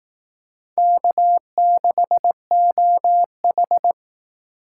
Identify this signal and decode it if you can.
Morse code, heard as a keyed tone, about 19 wpm, 700 Hz. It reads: K6OH